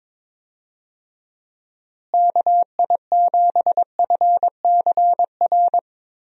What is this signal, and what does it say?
Morse code, heard as a keyed tone, about 22 wpm, 700 Hz. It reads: KI7FCR
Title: USA Call Signs - 6 Characters - Repeated 2x - 22wpm